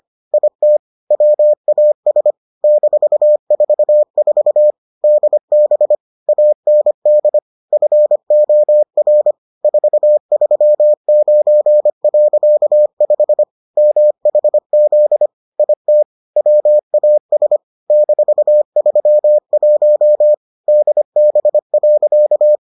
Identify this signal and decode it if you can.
Morse code, heard as a keyed tone, about 25 wpm, 600 Hz. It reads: IT WAS -44 DB AND FOR 439.5 MHZ IT WAS -31 DB.